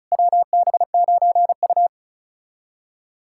Morse code, heard as a keyed tone, about 34 wpm, 700 Hz. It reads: WB9U
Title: USA Callsign DE Callsign - 4 Characters - 35wpm